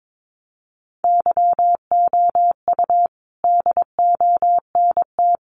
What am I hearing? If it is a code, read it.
Morse code, heard as a keyed tone, about 22 wpm, 700 Hz. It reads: YOU DONT
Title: Sentences from Top 1000 Words 22wpm